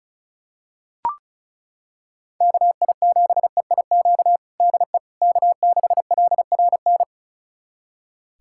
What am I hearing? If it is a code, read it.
Morse code, heard as a keyed tone, about 34 wpm, 700 Hz. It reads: KI7EIQ DE K6LRN
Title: USA Callsign DE Callsign 35wpm